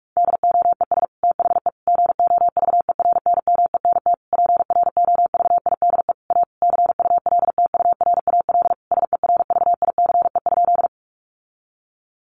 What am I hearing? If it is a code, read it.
Morse code, heard as a keyed tone, about 44 wpm, 700 Hz. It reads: DOES THE GOVERNMENT PROVIDE A CULTURAL SERVICE?
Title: Sentences from Top 1500 Words 45wpm